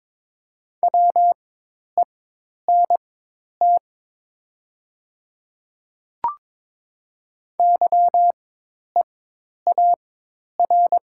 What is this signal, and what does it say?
Morse code, heard as a keyed tone, about 22 wpm, spaced as for 10 wpm, 700 Hz. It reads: WENT YEAR
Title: 4x Character Spacing - Top 200 Words - 22wpm